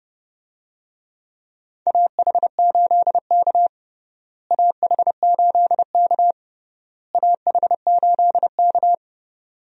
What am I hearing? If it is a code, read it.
Morse code, heard as a keyed tone, about 32 wpm, 700 Hz. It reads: AH8K AH8K AH8K
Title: USA Call Signs - 4 Characters - Repeated 3x - 30wpm